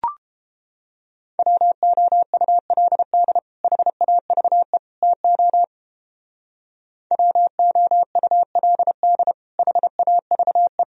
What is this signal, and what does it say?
Morse code, heard as a keyed tone, about 29 wpm, 700 Hz. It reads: WOULD HAVE TO WOULD HAVE